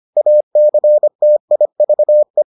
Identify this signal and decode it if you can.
Morse code, heard as a keyed tone, about 25 wpm, 600 Hz. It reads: ACTIVE